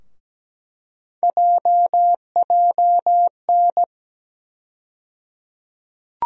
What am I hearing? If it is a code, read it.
Morse code, heard as a keyed tone, about 17 wpm, 700 Hz. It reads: JJN
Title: USA Call Sign Postfixes - 3 Letters 17wpm